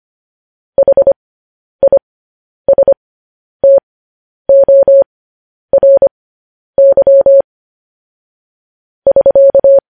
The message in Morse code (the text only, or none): HISTORY <SK>